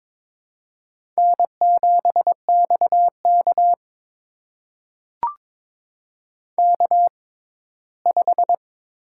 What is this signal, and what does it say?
Morse code, heard as a keyed tone, about 22 wpm, spaced as for 15 wpm, 700 Hz. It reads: N7XK K5
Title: USA Call Signs - 4 Characters - 6x Character Spacing - 22wpm